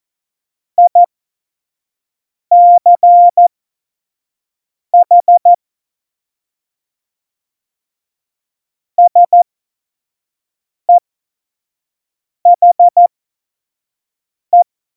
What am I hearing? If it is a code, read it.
Morse code, heard as a keyed tone, about 14 wpm, spaced as for 5 wpm, 700 Hz. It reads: ICH SEHE
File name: Morsekurs / Witze / Witz 06. Februar 2018, 25 F70